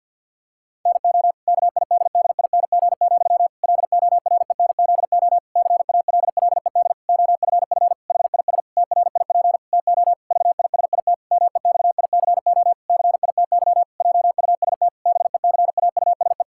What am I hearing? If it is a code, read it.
Morse code, heard as a keyed tone, about 50 wpm, 700 Hz. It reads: NO KIDDING, LORENZO CALLED OFF HIS TRIP TO VISIT MEXICO CITY JUST BECAUSE
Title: Pangram Challenge v2 - 50wpm